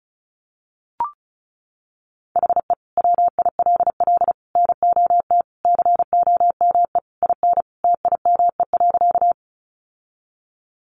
Morse code, heard as a keyed tone, about 34 wpm, 700 Hz. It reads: HE WILL NOT COME IN TIME.
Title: Sentences from Top 600 Words 35wpm